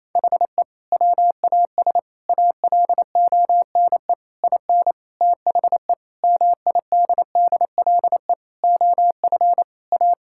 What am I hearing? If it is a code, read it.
Morse code, heard as a keyed tone, about 28 wpm, 700 Hz. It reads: HE WAS ALONE IN THE MIDDLE OF A
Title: Sentences from Top 1000 Words - Review Part 6 - 28wpm